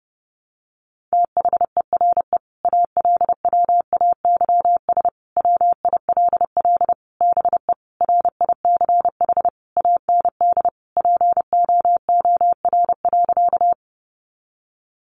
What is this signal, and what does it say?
Morse code, heard as a keyed tone, about 30 wpm, 700 Hz. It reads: THERE ALWAYS WILL BE RICH AND POOR.
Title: Sentences from Top 1100 Words 30wpm